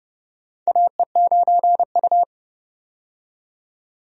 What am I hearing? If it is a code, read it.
Morse code, heard as a keyed tone, about 30 wpm, 700 Hz. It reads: AE9U
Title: USA Call Signs - 4 Characters - 30wpm